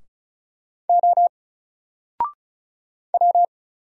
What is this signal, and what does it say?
Morse code, heard as a keyed tone, about 34 wpm, 700 Hz. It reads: O W
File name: Single-Letter-Number-35wpm